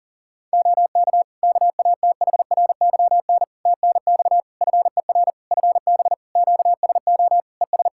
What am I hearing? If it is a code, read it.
Morse code, heard as a keyed tone, about 40 wpm, 700 Hz. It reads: OK KATHRYN TNX FER FB QSO ES
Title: K1USN SST QSO Practice 40wpm